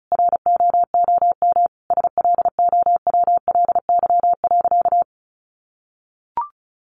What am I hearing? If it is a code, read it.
Morse code, heard as a keyed tone, about 34 wpm, 700 Hz. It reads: ROOM SLOWLY.
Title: Sentences from Top 1100 Words 35wpm